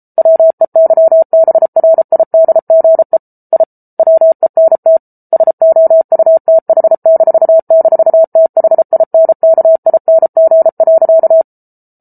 Morse code, heard as a keyed tone, about 34 wpm, 650 Hz. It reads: WEYBRIDGE I WENT SOUTH--THINKING.